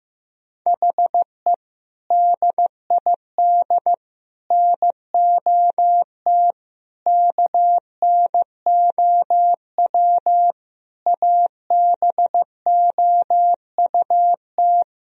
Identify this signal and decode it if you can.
Morse code, heard as a keyed tone, about 15 wpm, 700 Hz. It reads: HE DID NOT KNOW ABOUT
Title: Sentences from Top 1000 Words - Review Part 3 - 15wpm